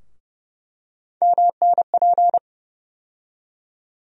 Morse code, heard as a keyed tone, about 31 wpm, 700 Hz. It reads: MNP
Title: USA Call Sign Postfixes - 3 Letters 30wpm